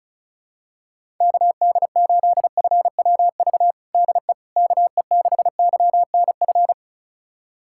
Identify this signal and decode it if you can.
Morse code, heard as a keyed tone, about 34 wpm, 700 Hz. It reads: KD8FWV DE KE6YNF
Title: USA Callsign DE Callsign 35wpm